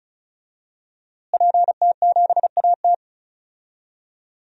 Morse code, heard as a keyed tone, about 34 wpm, 700 Hz. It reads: PT7AT
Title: DXCC-100 Call Signs - 5 Characters - Repeated 2x - 35wpm